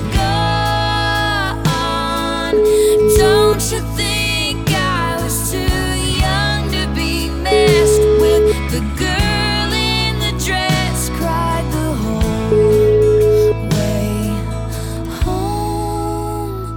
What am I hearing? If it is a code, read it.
Ringback tone (Europe, 425 Hz), with noise.